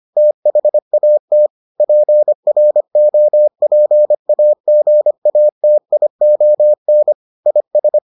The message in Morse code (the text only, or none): THAT PROPAGATION IS